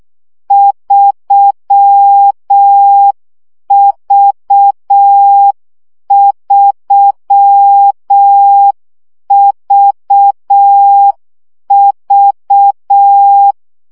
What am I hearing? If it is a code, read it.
Morse code, heard as a keyed tone, about 6 wpm, 800 Hz. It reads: 3V3VV
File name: CW gyakorlas v 3 5 baud